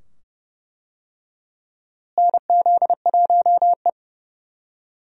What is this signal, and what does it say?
Morse code, heard as a keyed tone, about 32 wpm, 700 Hz. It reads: NZ1E
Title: USA Call Signs - 4 Characters - Repeated 2x - 30wpm